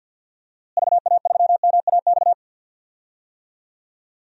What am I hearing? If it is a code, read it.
Morse code, heard as a keyed tone, about 52 wpm, 700 Hz. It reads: VA3MAX
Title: DXCC-100 Call Signs - 6 Characters - Repeated 2x - 50wpm